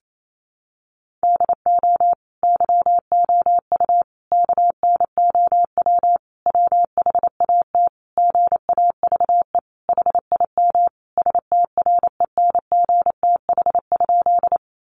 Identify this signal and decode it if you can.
Morse code, heard as a keyed tone, about 28 wpm, 700 Hz. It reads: DO YOU KNOW WHAT GAVE HIM STRENGTH?
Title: Sentences from Top 1300 Words 28wpm